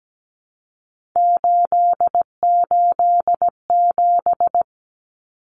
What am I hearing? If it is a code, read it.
Morse code, heard as a keyed tone, about 17 wpm, 700 Hz. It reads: 887
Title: Common Cut Numbers - 3-Digit 17wpm